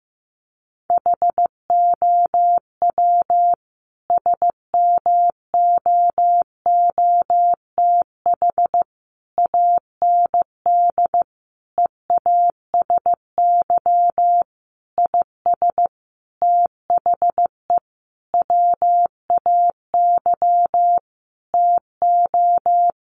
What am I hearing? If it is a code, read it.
Morse code, heard as a keyed tone, about 15 wpm, 700 Hz. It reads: HOW SMOOTH AND EASY IS THE WAY TO
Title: Sentences from Top 1400 Words 15wpm